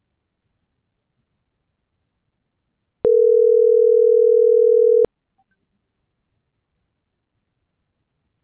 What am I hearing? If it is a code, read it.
Ringback tone (North America), 440 + 480 Hz.